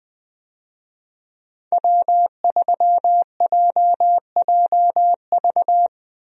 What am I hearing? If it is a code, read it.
Morse code, heard as a keyed tone, about 20 wpm, 700 Hz. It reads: W3JJV
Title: USA Call Signs - 5 Characters - 20wpm